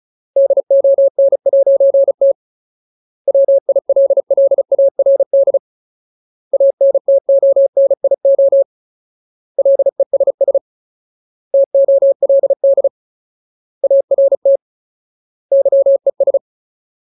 Morse code, heard as a keyed tone, about 34 wpm, 550 Hz. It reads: DON'T WILLARD ANTONIO LESS TOLD ART YES
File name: Ultimate Word List / 35WPM Ultimate